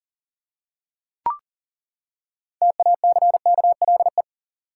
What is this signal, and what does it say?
Morse code, heard as a keyed tone, about 40 wpm, 700 Hz. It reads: TACKLE
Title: Six Letter Words - 40wpm